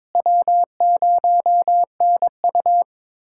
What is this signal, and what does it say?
Morse code, heard as a keyed tone, about 23 wpm, 700 Hz. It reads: W0NU
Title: USA Callsign DE Callsign - 4 Characters - 22wpm